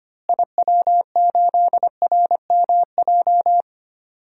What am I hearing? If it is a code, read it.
Morse code, heard as a keyed tone, about 25 wpm, 700 Hz. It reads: IW8RMJ
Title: DXCC-100 Call Signs - 6 Characters - Repeated 2x - 25wpm